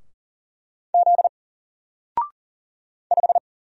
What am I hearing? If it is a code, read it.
Morse code, heard as a keyed tone, about 40 wpm, 700 Hz. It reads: Z 5
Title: Single Letter-Number 40wpm